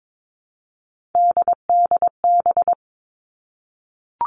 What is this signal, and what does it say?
Morse code, heard as a keyed tone, about 22 wpm, 700 Hz. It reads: DDB
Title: All Cut Numbers - 3-Digit 22wpm